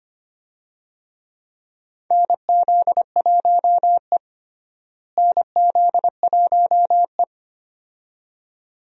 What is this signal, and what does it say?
Morse code, heard as a keyed tone, about 25 wpm, 700 Hz. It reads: NZ1E NZ1E
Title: USA Call Signs - 4 Characters - Repeated 2x - 25wpm